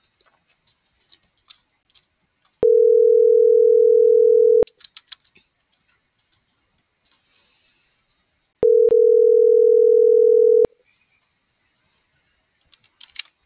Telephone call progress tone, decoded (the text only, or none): ringback tone